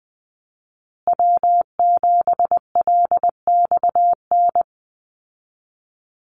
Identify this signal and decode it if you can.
Morse code, heard as a keyed tone, about 20 wpm, 700 Hz. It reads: W7LXN
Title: Call Signs Repeated 3x - 20wpm